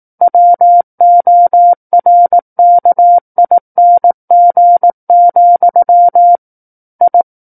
Morse code, heard as a keyed tone, about 18 wpm, 700 Hz. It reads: WORKING, I
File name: 18 WPM